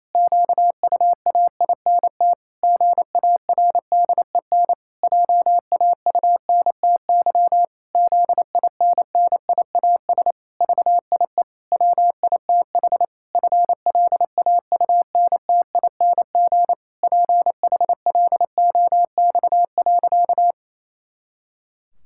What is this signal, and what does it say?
Morse code, heard as a keyed tone, about 28 wpm, 700 Hz. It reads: QUAINT GARDEN JAUNTY ZINNIAS VIE WITH FLAUNTING PHLOX.
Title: Pangram Challenge v2 - 28wpm